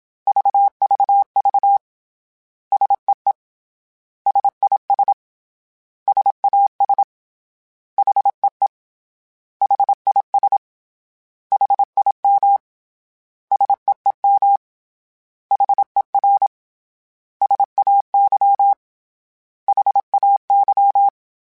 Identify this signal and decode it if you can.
Morse code, heard as a keyed tone, about 27 wpm, 800 Hz. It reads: VVV SEE SIS SAS HEE HIS HIM SEEM HER SAY HAY
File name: S vs H / S vs H mp3s / S vs H @25